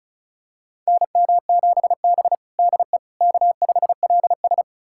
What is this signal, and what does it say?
Morse code, heard as a keyed tone, about 36 wpm, 700 Hz. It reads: NM7B DE K5LS
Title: USA Callsign DE Callsign - 4 Characters - 35wpm